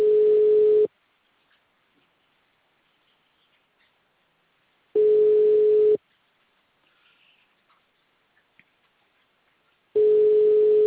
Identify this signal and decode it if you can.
Ringback tone (Europe, 425 Hz).